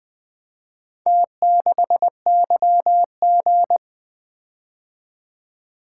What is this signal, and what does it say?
Morse code, heard as a keyed tone, about 24 wpm, 700 Hz. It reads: T6YG
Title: Speed-Racing - Four Letter-Number Combinations - 20wpm